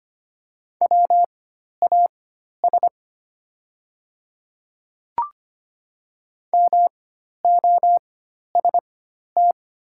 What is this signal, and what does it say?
Morse code, heard as a keyed tone, about 25 wpm, spaced as for 12 wpm, 700 Hz. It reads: WAS MOST
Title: 4x Character Spacing - Top 100 Words - 25wpm